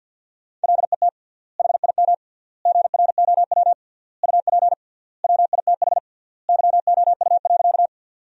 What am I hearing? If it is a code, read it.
Morse code, heard as a keyed tone, about 48 wpm, 700 Hz. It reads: LET HIM GROW UP WITH YOU.
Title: 3x Word-Spacing - Sentences from Top 300 Words - 50wpm